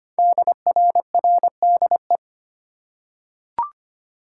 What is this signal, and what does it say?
Morse code, heard as a keyed tone, about 25 wpm, 700 Hz. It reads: DRRDE
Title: Code Groups - 5 Letters - 25wpm